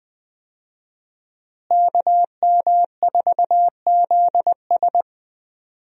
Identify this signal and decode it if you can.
Morse code, heard as a keyed tone, about 20 wpm, 700 Hz. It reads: KM4ZS
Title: USA Call Signs - 5 Characters - Repeated 2x - 20wpm